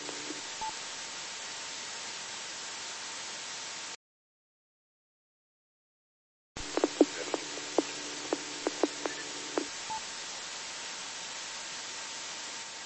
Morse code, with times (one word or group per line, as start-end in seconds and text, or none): none